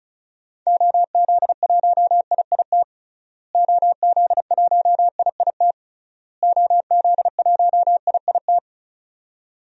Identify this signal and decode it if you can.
Morse code, heard as a keyed tone, about 34 wpm, 700 Hz. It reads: OZ1IIT OZ1IIT OZ1IIT
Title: DXCC-100 Call Signs - 6 Characters - Repeated 3x - 35wpm